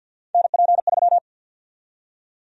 Morse code, heard as a keyed tone, about 52 wpm, 700 Hz. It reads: NP3
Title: USA Call Sign Prefixes - 50wpm